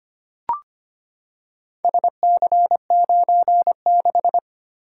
Morse code, heard as a keyed tone, about 25 wpm, 700 Hz. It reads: SC96
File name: practice / 25wpm / Four Letter-Number Combinations - 25wpm